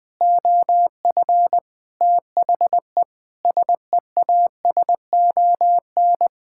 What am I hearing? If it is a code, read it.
Morse code, heard as a keyed tone, about 20 wpm, 700 Hz. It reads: OF THE SEASON
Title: Sets of 6 Words - 20wpm